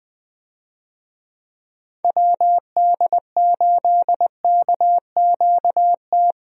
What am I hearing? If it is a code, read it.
Morse code, heard as a keyed tone, about 21 wpm, 700 Hz. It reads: WD8KQT
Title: USA Call Signs - 6 Characters - Repeated 3x - 20wpm